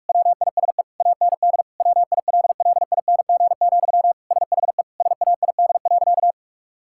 Morse code, heard as a keyed tone, about 46 wpm, 700 Hz. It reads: WISE AND WILLING, SHE SAID.
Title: Sentences from Top 1600 Words 45wpm